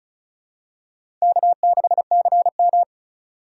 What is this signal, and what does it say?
Morse code, heard as a keyed tone, about 36 wpm, 690 Hz. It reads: K6CM
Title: USA Call Signs - 4 Characters - 2x Character Spacing - 35wpm